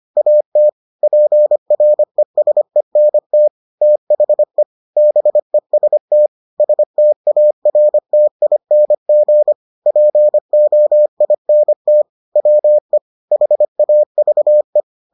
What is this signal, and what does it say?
Morse code, heard as a keyed tone, about 25 wpm, 600 Hz. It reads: AT PRESENT THE BEST STARTING POINT WE HAVE